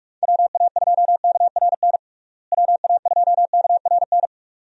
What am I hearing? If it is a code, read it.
Morse code, heard as a keyed tone, about 45 wpm, 700 Hz. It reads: WA2KRN WA2KRN
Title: USA Call Signs - 6 Characters - Repeated 2x - 45wpm